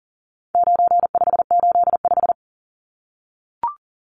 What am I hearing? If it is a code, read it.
Morse code, heard as a keyed tone, about 40 wpm, 700 Hz. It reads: 9585
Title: Numbers - 4-Digit 40wpm